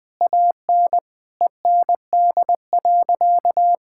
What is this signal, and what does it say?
Morse code, heard as a keyed tone, about 20 wpm, 700 Hz. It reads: AN END.
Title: Sentences from Top 1000 Words - Review Part 6 - 20wpm